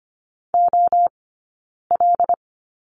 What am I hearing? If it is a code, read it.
Morse code, heard as a keyed tone, about 25 wpm, 700 Hz. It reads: O L
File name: Lesson 17d - Letter u - Two Characters 25wpm